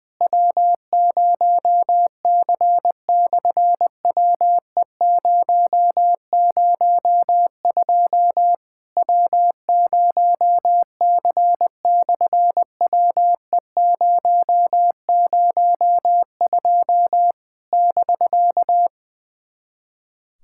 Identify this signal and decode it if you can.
Morse code, heard as a keyed tone, about 20 wpm, 700 Hz. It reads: W0C/WE002 W0C/WE002 <BK>
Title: SOTA QSO Practice 20wpm